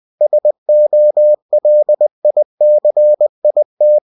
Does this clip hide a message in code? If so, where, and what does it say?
Yes: Morse code from 0.2 to 4.0 s, heard as a keyed tone, about 20 wpm, 600 Hz. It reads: SOLICIT